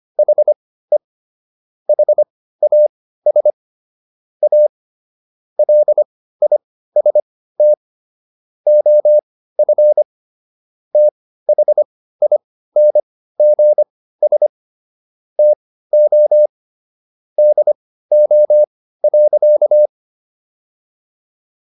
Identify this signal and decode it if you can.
Morse code, heard as a keyed tone, about 25 wpm, spaced as for 15 wpm, 600 Hz. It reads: HE HAS A LIST OF THINGS TO DO.